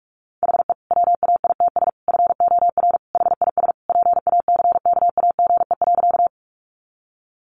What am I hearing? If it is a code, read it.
Morse code, heard as a keyed tone, about 44 wpm, 700 Hz. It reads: HE WAITS FOR HIS PACKAGE.
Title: Additional Easy Phrases 45wpm